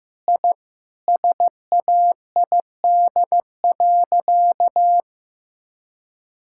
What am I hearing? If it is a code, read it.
Morse code, heard as a keyed tone, about 15 wpm, 700 Hz. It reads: I SAID.
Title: Sentences from Top 1000 Words - Review Part 6 - 15wpm